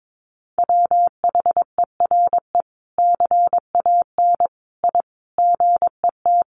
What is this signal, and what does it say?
Morse code, heard as a keyed tone, about 22 wpm, 700 Hz. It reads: WHERE CAN I GET
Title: Sentences from Top 1100 Words 22wpm